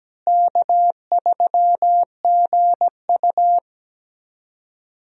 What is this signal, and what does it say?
Morse code, heard as a keyed tone, about 17 wpm, 700 Hz. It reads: K3GU
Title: USA Call Signs - 4 Characters - Repeated 3x - 17wpm